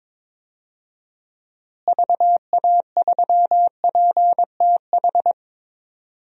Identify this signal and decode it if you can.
Morse code, heard as a keyed tone, about 22 wpm, 700 Hz. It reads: VA3PTH
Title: DXCC-100 Call Signs - 6 Characters - Repeated 2x - 22wpm